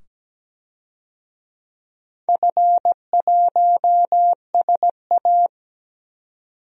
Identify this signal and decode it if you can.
Morse code, heard as a keyed tone, about 17 wpm, 700 Hz. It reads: F1SA
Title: DXCC-100 Call Signs - 4 Characters - 17wpm